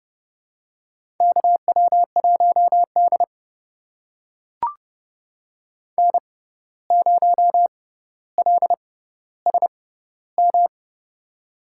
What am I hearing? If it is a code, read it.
Morse code, heard as a keyed tone, about 30 wpm, spaced as for 14 wpm, 700 Hz. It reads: KW1D N0LSM
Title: USA Call Signs - 6x Character Spacing - 30wpm